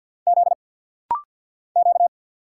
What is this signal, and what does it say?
Morse code, heard as a keyed tone, about 50 wpm, 700 Hz. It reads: Z Q
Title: Single Letters - Rapid-Fire 50wpm